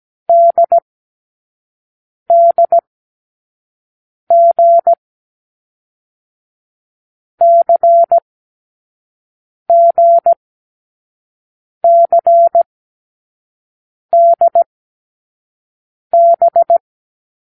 Morse code, heard as a keyed tone, about 17 wpm, 680 Hz. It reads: D D G C G C D B
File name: k7qo / Lesson12(A-H)